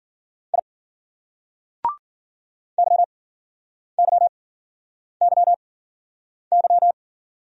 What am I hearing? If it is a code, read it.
Morse code, heard as a keyed tone, about 52 wpm, spaced as for 9 wpm, 700 Hz. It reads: I YYYY